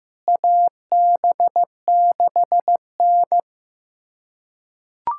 Morse code, heard as a keyed tone, about 15 wpm, 700 Hz. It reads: AB6N
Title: USA Callsign DE Callsign - 4 Characters - 15wpm